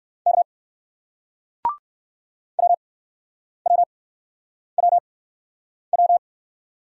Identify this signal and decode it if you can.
Morse code, heard as a keyed tone, about 59 wpm, spaced as for 9 wpm, 700 Hz. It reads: K WWWW